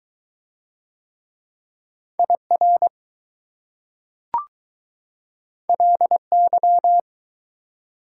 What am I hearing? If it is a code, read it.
Morse code, heard as a keyed tone, about 20 wpm, 700 Hz. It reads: IR LY